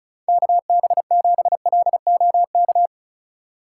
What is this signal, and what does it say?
Morse code, heard as a keyed tone, about 36 wpm, 700 Hz. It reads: KB7LOK
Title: USA Call Signs - 4x Character Spacing - 35wpm